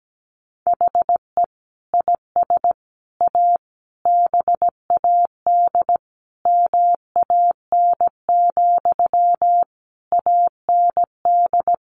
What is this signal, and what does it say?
Morse code, heard as a keyed tone, about 17 wpm, 700 Hz. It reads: HE IS A BAD MAN, AND
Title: Sentences from Top 1100 Words 17wpm